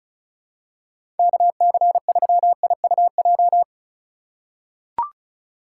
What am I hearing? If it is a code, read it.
Morse code, heard as a keyed tone, about 34 wpm, 700 Hz. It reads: KC3IUJ
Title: USA Call Signs - 6 Characters - 2x Character Spacing - 35wpm